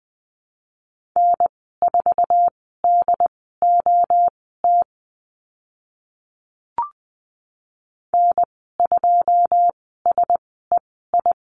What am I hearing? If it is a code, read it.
Morse code, heard as a keyed tone, about 20 wpm, spaced as for 14 wpm, 700 Hz. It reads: N4DOT N2SEI
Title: 082 Lesson 14g - Number 2 - Call Signs 20wpm